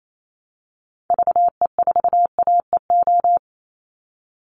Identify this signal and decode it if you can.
Morse code, heard as a keyed tone, about 28 wpm, 700 Hz. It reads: VE4AEO
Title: Call Signs Repeated 3x - 28wpm